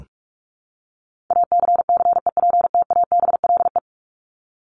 Morse code, heard as a keyed tone, about 44 wpm, 700 Hz. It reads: ACCEPTABLE
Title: Top 1501-1750 Words 45wpm